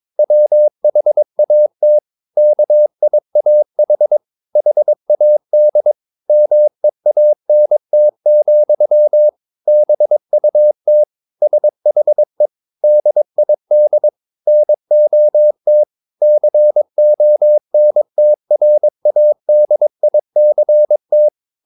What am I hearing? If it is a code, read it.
Morse code, heard as a keyed tone, about 22 wpm, 600 Hz. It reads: WHAT KIAH HAD MEANT, BUT SHE DID NOT CONTRADICT
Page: Romanen Never af Ken Follet på engelsk.